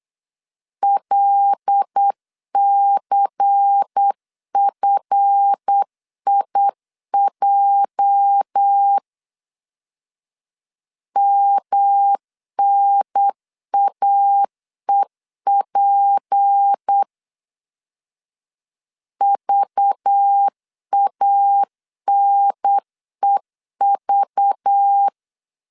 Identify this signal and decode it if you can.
Morse code, heard as a keyed tone, about 8 wpm, 800 Hz. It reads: LCFIJ MNAEP VANEV